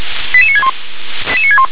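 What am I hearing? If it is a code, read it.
Five-tone selective call (ZVEI), heard as quick runs of five tones, tones of about 70 ms. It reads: 88961
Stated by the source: ZVEI five-tone selective call for address 88960, received by SDR.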